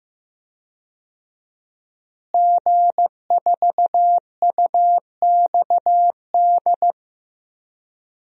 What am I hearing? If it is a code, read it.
Morse code, heard as a keyed tone, about 15 wpm, 700 Hz. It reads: G4UXD
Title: DXCC-100 Call Signs - 5 Characters - Repeated 2x - 15wpm